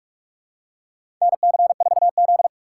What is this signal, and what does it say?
Morse code, heard as a keyed tone, about 46 wpm, 700 Hz. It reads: NC4Z
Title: USA Call Signs - 4 Characters - 2x Character Spacing - 45wpm